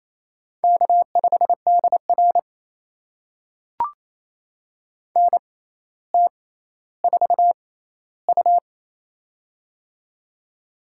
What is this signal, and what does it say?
Morse code, heard as a keyed tone, about 28 wpm, spaced as for 14 wpm, 700 Hz. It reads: K5DR NT4U
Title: USA Call Signs - 4 Characters - 6x Character Spacing - 28wpm